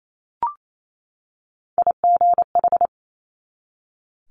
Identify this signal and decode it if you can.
Morse code, heard as a keyed tone, about 28 wpm, 700 Hz. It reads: IGH
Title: Multi-letter Phonemes 28wpm